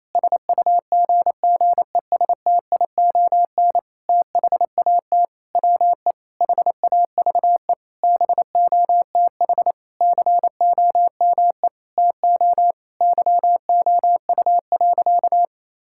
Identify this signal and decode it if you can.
Morse code, heard as a keyed tone, about 28 wpm, 700 Hz. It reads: SUGGESTION THAT WE HAVE BOTH COME TO YOU.
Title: Sentences from Top 1600 Words 28wpm